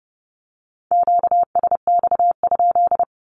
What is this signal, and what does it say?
Morse code, heard as a keyed tone, about 30 wpm, 700 Hz. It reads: QSX?
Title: Q-Codes 30wpm